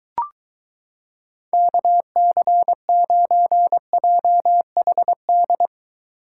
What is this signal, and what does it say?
Morse code, heard as a keyed tone, about 22 wpm, 700 Hz. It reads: KC9JHD